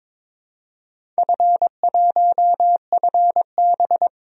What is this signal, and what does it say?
Morse code, heard as a keyed tone, about 22 wpm, 700 Hz. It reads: F1FB
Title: DXCC-100 Call Signs - 4 Characters - Repeated 3x - 22wpm